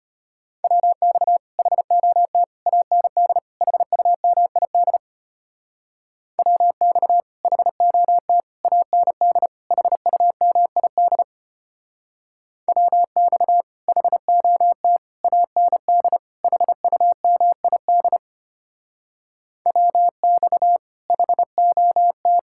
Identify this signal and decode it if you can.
Morse code, heard as a keyed tone, about 33 wpm, 700 Hz. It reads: WX HOT AND HUMID WX HOT AND HUMID WX HOT AND HUMID WX HOT